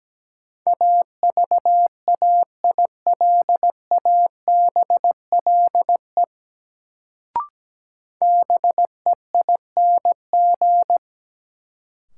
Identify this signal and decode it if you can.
Morse code, heard as a keyed tone, about 17 wpm, 700 Hz. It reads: AVAILABLE BEING
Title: Sentences from Top 600 Words - New Word Drill - 17wpm